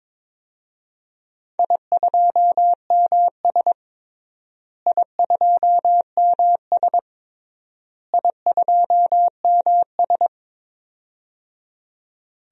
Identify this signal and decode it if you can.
Morse code, heard as a keyed tone, about 22 wpm, 700 Hz. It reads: I2MS I2MS I2MS
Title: DXCC-100 Call Signs - 4 Characters - Repeated 3x - 22wpm